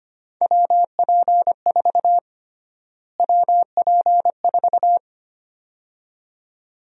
Morse code, heard as a keyed tone, about 25 wpm, 700 Hz. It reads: WP4 WP4
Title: USA Call Sign Prefixes - Repeated 2x - 25wpm